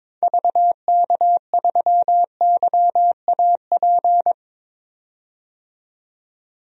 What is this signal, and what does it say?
Morse code, heard as a keyed tone, about 21 wpm, 700 Hz. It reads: VK3YAP
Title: DXCC-100 Call Signs - 6 Characters - 22wpm